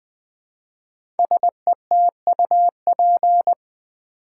Morse code, heard as a keyed tone, about 20 wpm, 700 Hz. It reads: SETUP
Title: 5 Letter Words - First 3 Letters Given - 20wpm